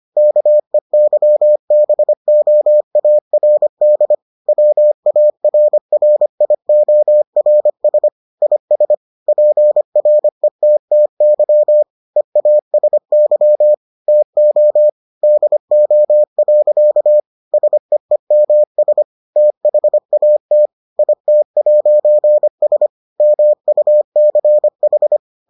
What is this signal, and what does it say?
Morse code, heard as a keyed tone, about 25 wpm, 600 Hz. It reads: KEYBOARD WARRIORS IS PRETTY EASY TO DO. SEEMS THAT IT'S MUCH